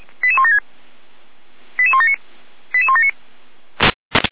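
Five-tone selective call (ZVEI), heard as quick runs of five tones, tones of about 70 ms. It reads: 88176 88179 88179
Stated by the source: ZVEI five-tone selective call received on an SDR.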